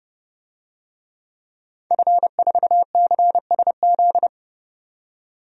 Morse code, heard as a keyed tone, about 30 wpm, 700 Hz. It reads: F4CSZ
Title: DXCC-100 Call Signs - 5 Characters - 30wpm